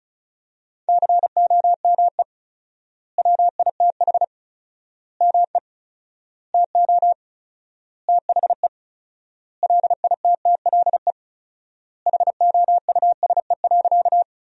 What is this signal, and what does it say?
Morse code, heard as a keyed tone, about 36 wpm, 700 Hz. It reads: COME WITH ME TO THE LITTLE HOUSE.
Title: 4x Word-Spacing - Sentences from Top 200 Words - 35wpm